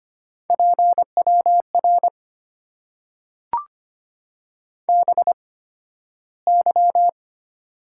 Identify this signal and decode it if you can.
Morse code, heard as a keyed tone, about 25 wpm, spaced as for 13 wpm, 690 Hz. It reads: PWR BY
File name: Top 90 QSO Elements - 8x Character Spacing - 25wpm